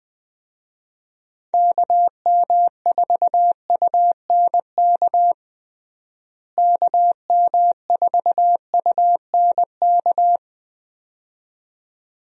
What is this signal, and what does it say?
Morse code, heard as a keyed tone, about 21 wpm, 700 Hz. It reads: KM4UNK KM4UNK
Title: USA Call Signs - 6 Characters - Repeated 2x - 20wpm